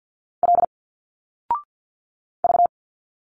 Morse code, heard as a keyed tone, about 49 wpm, 700 Hz. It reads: L V
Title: Single Letters 50wpm